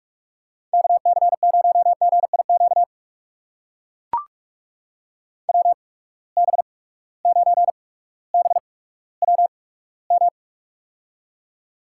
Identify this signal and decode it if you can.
Morse code, heard as a keyed tone, about 44 wpm, spaced as for 19 wpm, 700 Hz. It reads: KC0GIQ WB9BWM
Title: USA Call Signs - 6 Characters - 8x Character Spacing - 45wpm